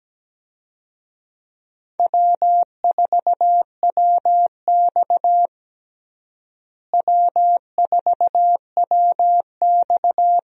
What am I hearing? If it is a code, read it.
Morse code, heard as a keyed tone, about 17 wpm, 700 Hz. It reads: W4WX W4WX
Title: USA Call Signs - 4 Characters - Repeated 3x - 17wpm